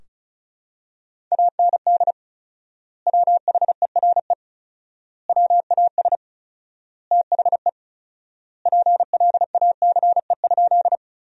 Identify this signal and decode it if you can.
Morse code, heard as a keyed tone, about 34 wpm, 700 Hz. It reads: AND WHERE WAS THE PLACE?
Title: 4x Word-Spacing - Sentences from Top 200 Words - 35wpm